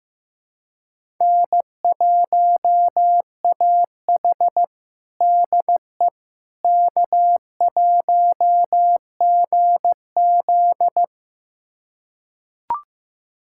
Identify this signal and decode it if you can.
Morse code, heard as a keyed tone, about 15 wpm, 700 Hz. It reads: N1AH DE K1GZ
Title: USA Callsign DE Callsign - 4 Characters - 15wpm